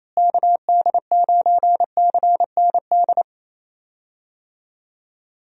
Morse code, heard as a keyed tone, about 27 wpm, 700 Hz. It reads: KD9CND
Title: USA Call Signs - 6 Characters - Repeated 2x - 28wpm